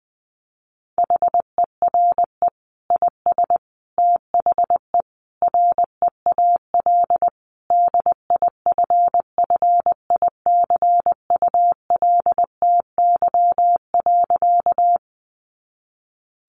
Morse code, heard as a keyed tone, about 20 wpm, 700 Hz. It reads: HERE IS THE REAL DIFFICULTY.
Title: Sentences from Top 1500 Words 20wpm